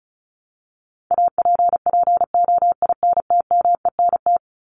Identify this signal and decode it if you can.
Morse code, heard as a keyed tone, about 34 wpm, 700 Hz. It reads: APPOINTMENT
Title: Top 1751-2000 Words 35wpm